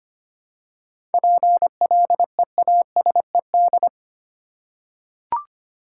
Morse code, heard as a keyed tone, about 25 wpm, 700 Hz. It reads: PLEASED
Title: Seven Letter Words - 25wpm